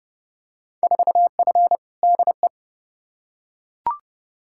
Morse code, heard as a keyed tone, about 29 wpm, 700 Hz. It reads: 4F DE
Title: Field Day Exchanges Encore - 30wpm